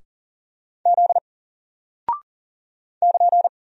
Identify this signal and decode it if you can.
Morse code, heard as a keyed tone, about 40 wpm, 700 Hz. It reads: Z <KN>
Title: Single Character (Letter:Number:Punctuation:Prosign) Alt 40wpm